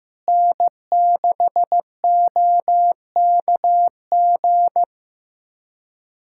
Morse code, heard as a keyed tone, about 15 wpm, 700 Hz. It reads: N6OKG
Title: USA Callsign DE Callsign - 5 Characters - 15wpm